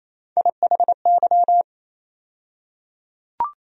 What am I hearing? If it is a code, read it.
Morse code, heard as a keyed tone, about 28 wpm, 700 Hz. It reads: IHY
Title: USA Call Sign Postfixes - 3 Letters 28wpm